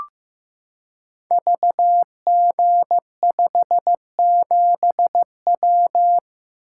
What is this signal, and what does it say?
Morse code, heard as a keyed tone, about 15 wpm, 700 Hz. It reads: VG57W
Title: Code Groups - 5 Letter:Numbers - 15wpm